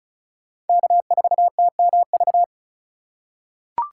Morse code, heard as a keyed tone, about 36 wpm, 700 Hz. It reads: K4TMV
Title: USA Call Signs - 5 Characters - 3x Character Spacing - 35wpm